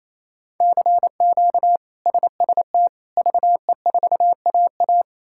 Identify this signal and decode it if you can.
Morse code, heard as a keyed tone, about 28 wpm, 700 Hz. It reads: CQ SST VE4AA
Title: K1USN SST QSO Practice - 28wpm